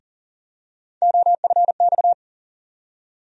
Morse code, heard as a keyed tone, about 40 wpm, 700 Hz. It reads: OFX
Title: USA Call Sign Postfixes - 3 Letters 40wpm